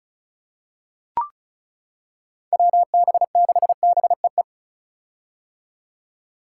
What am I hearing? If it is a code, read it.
Morse code, heard as a keyed tone, about 36 wpm, 700 Hz. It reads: WB6BEE
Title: K1USN SST QSO Practice - 35wpm